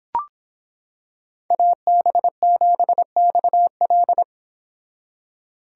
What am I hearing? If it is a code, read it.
Morse code, heard as a keyed tone, about 26 wpm, 700 Hz. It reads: AB7XL